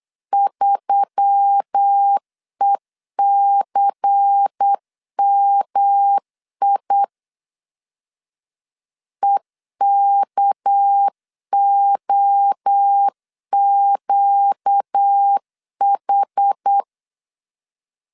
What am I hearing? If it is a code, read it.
Morse code, heard as a keyed tone, about 8 wpm, 800 Hz. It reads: 3ECMI EKOQH
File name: LEZIONI / 6 PKX0 / 6 PKX0 8wpm 3341